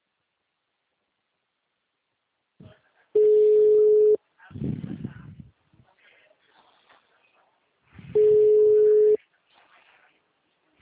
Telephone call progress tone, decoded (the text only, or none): ringback tone